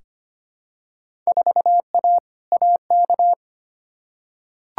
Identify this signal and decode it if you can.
Morse code, heard as a keyed tone, about 25 wpm, 700 Hz. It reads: 4A AK
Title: Field Day Exchanges Encore - 25wpm